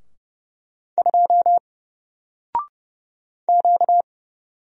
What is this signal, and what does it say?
Morse code, heard as a keyed tone, about 30 wpm, 700 Hz. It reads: 2 Q
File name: Single-Letter-Number-30wpm